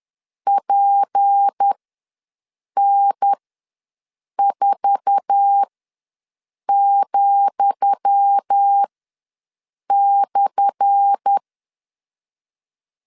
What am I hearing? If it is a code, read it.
Morse code, heard as a keyed tone, about 10 wpm, spaced as for 6 wpm, 800 Hz. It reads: PN4,/
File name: LEZIONI / 10 SIGNS / 10D COMA ws 10wpm 4314